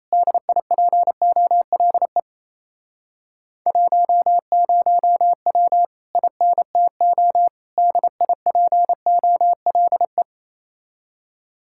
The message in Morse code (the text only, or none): DIPOLE 10W INTO DIPOLE